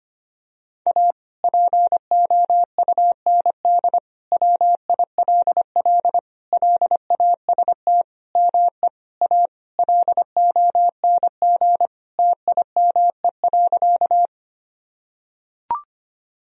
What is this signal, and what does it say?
Morse code, heard as a keyed tone, about 25 wpm, 700 Hz. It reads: A POUND WILL LAST ME A LONG TIME.
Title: Sentences from Top 1000 Words - Review Part 2 - 25wpm